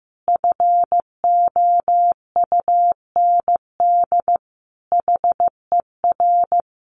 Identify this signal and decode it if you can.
Morse code, heard as a keyed tone, about 15 wpm, 700 Hz. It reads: FOUND HER
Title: Sentences from Top 1200 Words 15wpm